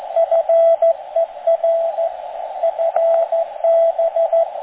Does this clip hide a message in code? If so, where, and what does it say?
Yes: Morse code from 0.2 to 4.4 s, heard as a keyed tone, about 14 wpm, 660 Hz, with radio noise. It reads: FER FB